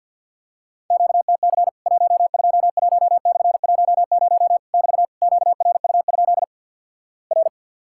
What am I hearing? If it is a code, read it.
Morse code, heard as a keyed tone, about 50 wpm, 700 Hz. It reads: QTC 121/10 <BT> QRU? R
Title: Worked All Europe DX Contest - QTC Exchange - 50wpm